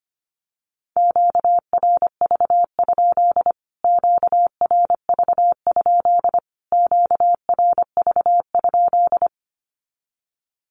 Morse code, heard as a keyed tone, about 25 wpm, 700 Hz. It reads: QRV? QRV? QRV?
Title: Q-Codes Repeated 3x 25wpm